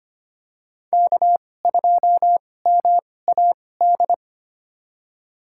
Morse code, heard as a keyed tone, about 25 wpm, spaced as for 18 wpm, 700 Hz. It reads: K2MAD
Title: USA Call Signs - 2x Character Spacing - 25wpm